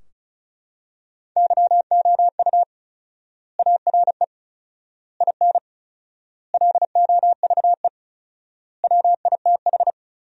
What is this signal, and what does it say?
Morse code, heard as a keyed tone, about 36 wpm, 700 Hz. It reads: YOU ARE IN LOVE WITH
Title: 4x Word-Spacing - Sentences from Top 400 Words - 35wpm